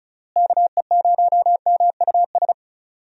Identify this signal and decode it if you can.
Morse code, heard as a keyed tone, about 34 wpm, 700 Hz. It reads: KE0MUS
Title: USA Call Signs - 6 Characters - Repeated 3x - 35wpm